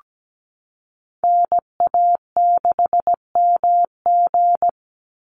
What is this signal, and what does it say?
Morse code, heard as a keyed tone, about 17 wpm, 700 Hz. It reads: NA6MG
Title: NA SOTA Activator Callsigns 17wpm